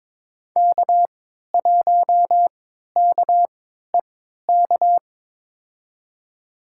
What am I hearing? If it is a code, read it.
Morse code, heard as a keyed tone, about 21 wpm, spaced as for 12 wpm, 700 Hz. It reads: K1KEK
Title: USA Call Signs - 3x Character Spacing - 22wpm